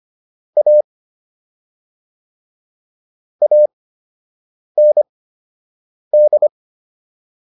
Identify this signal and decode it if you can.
Morse code, heard as a keyed tone, about 25 wpm, spaced as for 7 wpm, 600 Hz. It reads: A AND